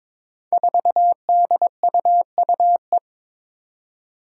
Morse code, heard as a keyed tone, about 22 wpm, 700 Hz. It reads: 4DUUE
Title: Five Challenging Letter-Number Combinations 22wpm